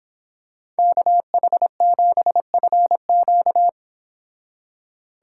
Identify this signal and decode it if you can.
Morse code, heard as a keyed tone, about 25 wpm, 700 Hz. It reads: KH7FQ
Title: Speed Racing - USA 2x2 Call Signs 22wpm